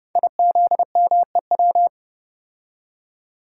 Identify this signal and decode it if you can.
Morse code, heard as a keyed tone, about 30 wpm, 700 Hz. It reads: IZMEW
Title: Code Groups - 5 Letters - 30wpm